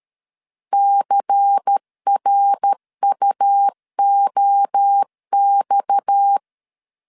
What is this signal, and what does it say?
Morse code, heard as a keyed tone, about 13 wpm, 800 Hz. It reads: CRUOX
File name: LEZIONI / 6 PKX0 / 6 PKX 12wpm 2024